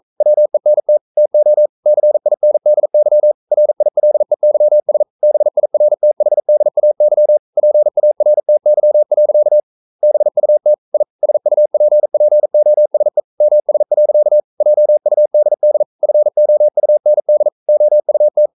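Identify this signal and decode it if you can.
Morse code, heard as a keyed tone, about 42 wpm, 600 Hz. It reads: WENT TO CINDY RILEYS BIRTHDAY PARTY. BUT I SUPPOSE MS. JUDD FOUND OUT
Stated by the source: Morse code sending of an English novel at 42 wpm.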